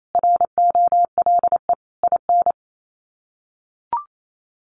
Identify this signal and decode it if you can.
Morse code, heard as a keyed tone, about 28 wpm, 700 Hz. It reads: ROLE IN
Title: Sets of 5 Words Encore 28wpm